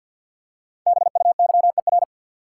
Morse code, heard as a keyed tone, about 52 wpm, 700 Hz. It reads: BUYER
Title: Five Letter Words - 50wpm